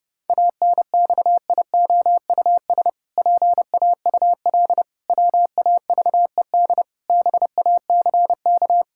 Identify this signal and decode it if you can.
Morse code, heard as a keyed tone, about 32 wpm, 700 Hz. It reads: ANXIOUS PAUL WAVED BACK
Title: Pangram Challenge v2 - 30wpm